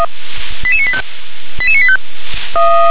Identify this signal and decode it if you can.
Five-tone selective call (ZVEI), heard as quick runs of five tones, tones of about 70 ms. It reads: 88985 88985